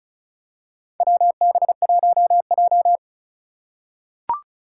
Morse code, heard as a keyed tone, about 36 wpm, 700 Hz. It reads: WB1J
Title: USA Call Signs - 4 Characters - 4x Character Spacing - 35wpm